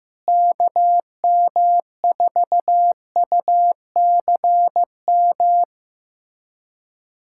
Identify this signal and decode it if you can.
Morse code, heard as a keyed tone, about 15 wpm, 700 Hz. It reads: KM4UCM
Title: USA Callsign DE Callsign - 6 Characters - 15wpm